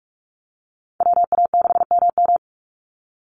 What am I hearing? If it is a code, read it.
Morse code, heard as a keyed tone, about 44 wpm, 700 Hz. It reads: WA6MM
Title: NA SOTA Chaser Callsigns 45wpm